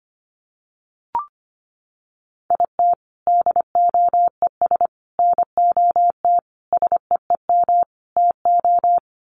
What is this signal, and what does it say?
Morse code, heard as a keyed tone, about 25 wpm, 700 Hz. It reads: IT DOES NOT SEEM TO
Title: Sets of 5 Words Encore2 25wpm